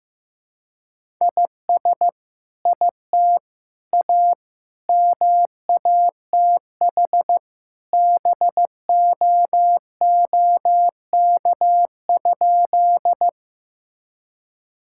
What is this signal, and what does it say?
Morse code, heard as a keyed tone, about 15 wpm, 700 Hz. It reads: IS IT A MATH BOOK?
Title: Sentences from Top 1000 Words - Review Part 6 - 15wpm